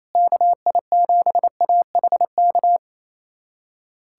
Morse code, heard as a keyed tone, about 28 wpm, 700 Hz. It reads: KI7AHK
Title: USA Callsign DE Callsign 28wpm